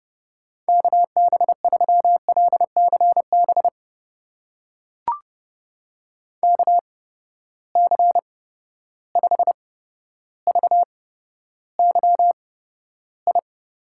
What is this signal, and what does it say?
Morse code, heard as a keyed tone, about 32 wpm, spaced as for 13 wpm, 700 Hz. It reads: KB3LCB KC5VYI
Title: USA Call Signs - 6 Characters - 8x Character Spacing - 30wpm